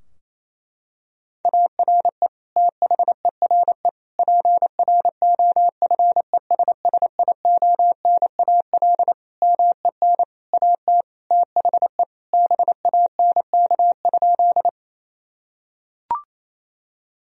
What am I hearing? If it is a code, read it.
Morse code, heard as a keyed tone, about 27 wpm, 700 Hz. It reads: ARE THERE PROFESSIONAL MEN AT THE BANK?
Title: Sentences from Top 1000 Words - Review Part 4 - 28wpm